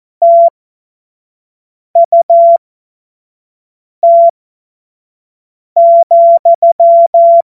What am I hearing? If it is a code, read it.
Morse code, heard as a keyed tone, about 14 wpm, spaced as for 5 wpm, 680 Hz. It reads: TUT,
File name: Morsekurs / Witze / Witz 15. November 2022 25 F70